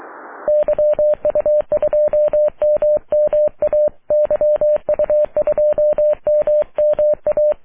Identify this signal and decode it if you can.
Morse code, heard as a keyed tone, about 23 wpm, 600 Hz. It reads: YU2MMA YU2MMA
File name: wp / la8aja yu2mma aurora 05 aug 2011